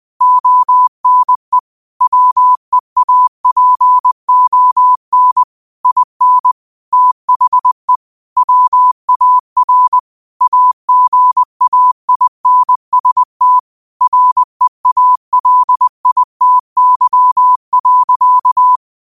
Morse code, heard as a keyed tone, about 20 wpm, 1000 Hz. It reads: ONE WEAPON IN THE WAR AGAINST REALITY.